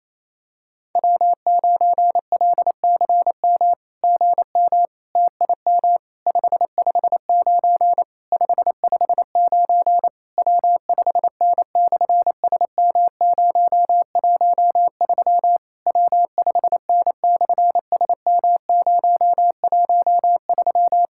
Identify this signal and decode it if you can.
Morse code, heard as a keyed tone, about 27 wpm, 700 Hz. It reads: W9LCM GM TIM 559 559 W5N/SM013 W5N/SM013
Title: SOTA QSO Practice 28wpm